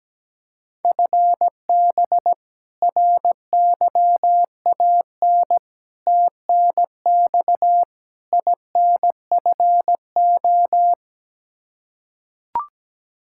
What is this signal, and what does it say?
Morse code, heard as a keyed tone, about 17 wpm, 700 Hz. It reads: FB RYAN TNX INFO
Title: K1USN SST QSO Practice 17wpm